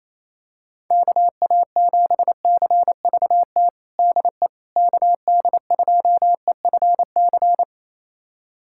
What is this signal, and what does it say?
Morse code, heard as a keyed tone, about 28 wpm, 700 Hz. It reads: KA7CVT DE KD2EFC
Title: USA Callsign DE Callsign 28wpm